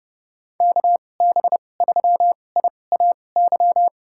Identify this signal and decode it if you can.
Morse code, heard as a keyed tone, about 30 wpm, spaced as for 22 wpm, 700 Hz. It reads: KB3IAY
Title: USA Call Signs - 6 Characters - 2x Character Spacing - 30wpm